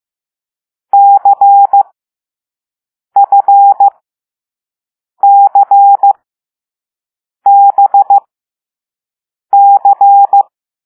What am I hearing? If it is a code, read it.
Morse code, heard as a keyed tone, about 15 wpm, spaced as for 5 wpm, 800 Hz. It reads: CFCBC
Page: Play Lesson 6 C - all letters in previous lessons